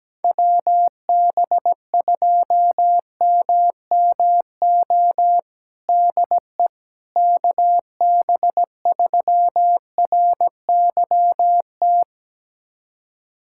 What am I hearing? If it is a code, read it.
Morse code, heard as a keyed tone, about 17 wpm, 700 Hz. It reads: WB2MMO DE KB3RYT
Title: USA Callsign DE Callsign - 6 Characters - 17wpm